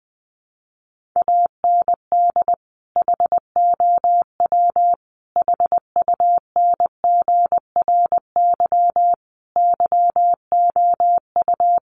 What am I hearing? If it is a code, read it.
Morse code, heard as a keyed tone, about 20 wpm, 700 Hz. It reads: AND HOW HUNGRY YOU
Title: Sentences from Top 1400 Words 20wpm